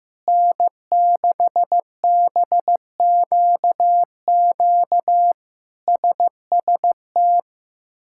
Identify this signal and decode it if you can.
Morse code, heard as a keyed tone, about 15 wpm, 700 Hz. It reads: N6BQQ SST
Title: K1USN SST QSO Practice - 15wpm